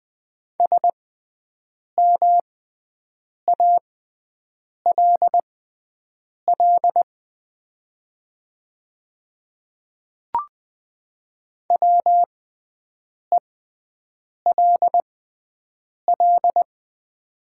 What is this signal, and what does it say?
Morse code, heard as a keyed tone, about 20 wpm, spaced as for 7 wpm, 700 Hz. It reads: SMALL WELL
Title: 6x Character Spacing - Top 300 Words - 20wpm